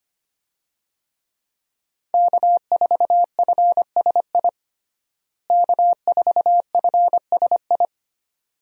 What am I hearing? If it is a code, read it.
Morse code, heard as a keyed tone, about 25 wpm, 700 Hz. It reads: K4FSI K4FSI
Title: USA Call Signs - 5 Characters - Repeated 2x - 25wpm